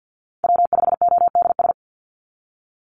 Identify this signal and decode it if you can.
Morse code, heard as a keyed tone, about 50 wpm, 700 Hz. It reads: W5ODS
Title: NA SOTA Activator Callsigns 50wpm